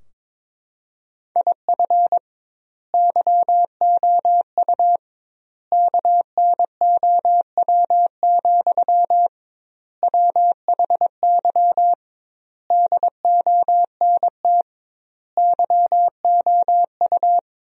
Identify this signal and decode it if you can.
Morse code, heard as a keyed tone, about 22 wpm, 700 Hz. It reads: IF YOU KNOW, WHY DONT YOU
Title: 2x Word-Spacing - Sentences from Top 300 Words - 22wpm